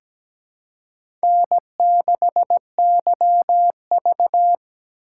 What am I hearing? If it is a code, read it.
Morse code, heard as a keyed tone, about 17 wpm, 700 Hz. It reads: N6YV
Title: USA Call Signs - 4 Characters - 3x Character Spacing - 17wpm